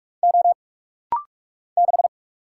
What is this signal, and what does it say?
Morse code, heard as a keyed tone, about 44 wpm, 700 Hz. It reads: O 6
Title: Single Letter-Number Alt - Rapid-Fire 45wpm